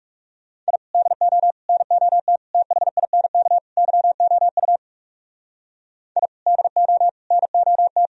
Morse code, heard as a keyed tone, about 36 wpm, 700 Hz. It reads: I DO NOT THINK YOU I DO NOT
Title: Speed-Racing - Sets of 5 Words 30wpm